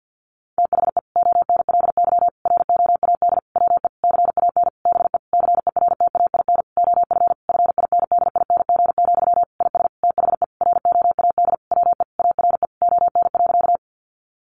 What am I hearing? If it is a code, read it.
Morse code, heard as a keyed tone, about 50 wpm, 700 Hz. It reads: THE ONLY ROAD WE CAN BE CERTAIN OF FINDING, IS THE ROAD WE ARE ON.
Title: Sentences from Top 1100 Words 50wpm